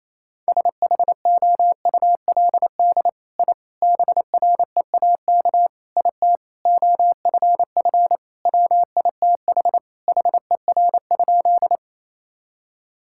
Morse code, heard as a keyed tone, about 28 wpm, 700 Hz. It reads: SHOULD I BREAK IT OFF WITH HER?
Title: Sentences from Top 1000 Words - Review Part 6 - 28wpm